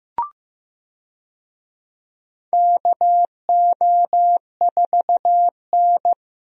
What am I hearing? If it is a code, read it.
Morse code, heard as a keyed tone, about 15 wpm, 700 Hz. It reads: KO4N
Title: USA Call Signs - 4 Characters - Repeated 3x - 15wpm